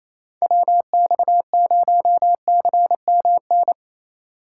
Morse code, heard as a keyed tone, about 28 wpm, 700 Hz. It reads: WX0CMN
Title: USA Call Signs - 6 Characters - 4x Character Spacing - 28wpm